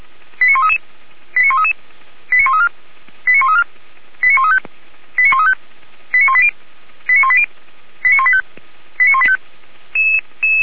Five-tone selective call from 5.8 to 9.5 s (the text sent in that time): 89180 89180 89185 89185